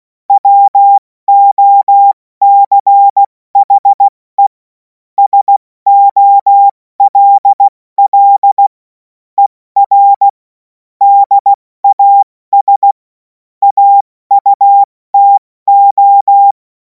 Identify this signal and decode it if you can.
Morse code, heard as a keyed tone, about 16 wpm, 800 Hz. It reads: WOCHE SOLL ER DAS AUTO
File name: Morsekurs / Witze / Witz 28. April 2026 70 F80